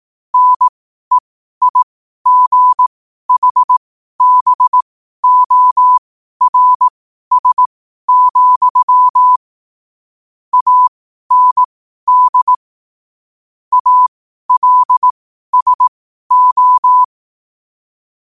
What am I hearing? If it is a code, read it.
Morse code, heard as a keyed tone, about 18 wpm, spaced as for 12 wpm, 1000 Hz. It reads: NEIGHBORS, AND ALSO
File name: qotdfiles / 12wpm